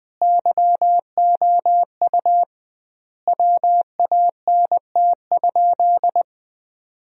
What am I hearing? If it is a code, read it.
Morse code, heard as a keyed tone, about 20 wpm, 700 Hz. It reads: YOU WANT?
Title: 2x Word-Spacing - Sentences from Top 200 Words - 20wpm